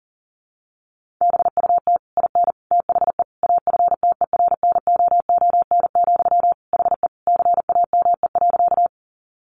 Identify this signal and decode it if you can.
Morse code, heard as a keyed tone, about 37 wpm, 700 Hz. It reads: BUT IN THE AFTERNOON, HE CAME.
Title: Sentences from Top 1500 Words 40wpm